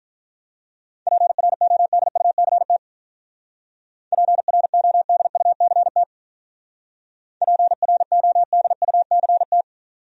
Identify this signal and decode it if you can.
Morse code, heard as a keyed tone, about 44 wpm, 700 Hz. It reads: PRODUCT PRODUCT PRODUCT